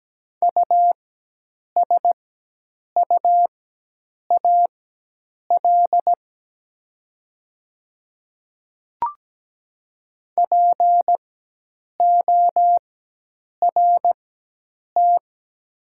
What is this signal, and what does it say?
Morse code, heard as a keyed tone, about 17 wpm, spaced as for 8 wpm, 700 Hz. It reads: USUAL PORT
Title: 4x Character Spacing - Top 500 Words - 17wpm